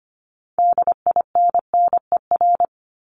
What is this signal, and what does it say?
Morse code, heard as a keyed tone, about 25 wpm, 700 Hz. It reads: DINNER
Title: Multi-letter Phonemes 25wpm